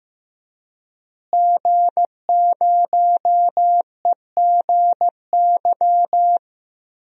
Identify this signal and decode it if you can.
Morse code, heard as a keyed tone, about 15 wpm, 700 Hz. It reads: G0EGY
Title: DXCC-100 Call Signs - 5 Characters - Repeated 3x - 15wpm